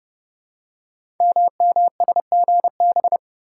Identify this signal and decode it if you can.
Morse code, heard as a keyed tone, about 29 wpm, 700 Hz. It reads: MMSGB
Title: Code Groups - 5 Letters - 30wpm